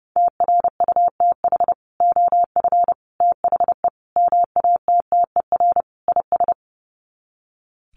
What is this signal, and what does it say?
Morse code, heard as a keyed tone, about 30 wpm, 700 Hz. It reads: TRUTH OF THE MATTER IS
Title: Sets of 5 Words Encore2 30wpm